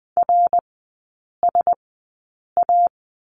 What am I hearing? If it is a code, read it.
Morse code, heard as a keyed tone, about 20 wpm, spaced as for 8 wpm, 700 Hz. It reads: RSA
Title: 053 Lesson 10e - Letter r - Three Characters 20wpm